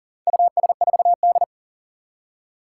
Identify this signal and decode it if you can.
Morse code, heard as a keyed tone, about 40 wpm, 700 Hz. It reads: US4D
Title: Four Challenging Letter-Number Combinations 40wpm